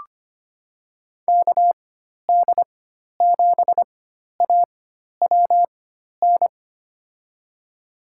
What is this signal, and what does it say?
Morse code, heard as a keyed tone, about 25 wpm, spaced as for 12 wpm, 700 Hz. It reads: KD7AWN
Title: USA Call Signs - 6 Characters - 4x Character Spacing - 25wpm